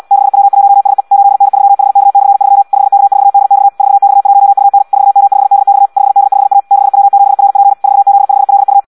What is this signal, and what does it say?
Morse code, heard as a keyed tone, about 74 wpm, 800 Hz. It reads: 6W1SE 8N3HK/3 5K4DX 4X1UN 4M5DX 5D5A 6W7RP 4Z5LU
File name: Audio practica CW / qrq-tx-74 2